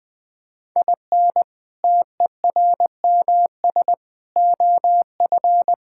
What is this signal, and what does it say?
Morse code, heard as a keyed tone, about 20 wpm, 700 Hz. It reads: IN TERMS OF
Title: Sentences with 4-Word Phrases - Spoken Beginning - 20wpm